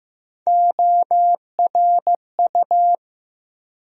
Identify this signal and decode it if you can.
Morse code, heard as a keyed tone, about 15 wpm, 700 Hz. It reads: ORU
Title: USA Call Sign Postfixes - 3 Letters 15wpm